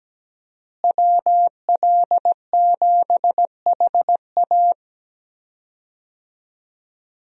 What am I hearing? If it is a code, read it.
Morse code, heard as a keyed tone, about 17 wpm, 700 Hz. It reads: WL7HA
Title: USA Call Signs - 5 Characters - Repeated 2x - 17wpm